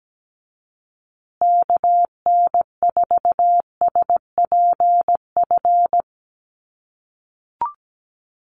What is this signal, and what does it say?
Morse code, heard as a keyed tone, about 17 wpm, 700 Hz. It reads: KN4SPF
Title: USA Call Signs Repeated 3x 17wpm